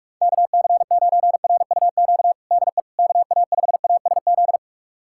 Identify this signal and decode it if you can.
Morse code, heard as a keyed tone, about 46 wpm, 700 Hz. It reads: KC9RUQ DE KA5ASZ
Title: USA Callsign DE Callsign - 6 Characters - 45wpm